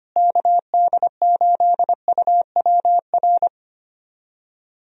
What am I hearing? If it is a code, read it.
Morse code, heard as a keyed tone, about 25 wpm, 700 Hz. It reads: KD8UWR
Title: USA Call Signs - 6 Characters - Repeated 2x - 25wpm